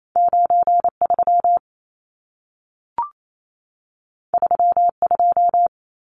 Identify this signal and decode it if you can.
Morse code, heard as a keyed tone, about 28 wpm, 700 Hz. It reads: 93 32
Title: Numbers - 2-Digit 28wpm